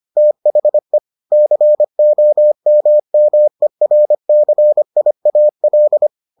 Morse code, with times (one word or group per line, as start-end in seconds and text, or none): 0.2-1.0 THE
1.3-6.1 COMMERCIAL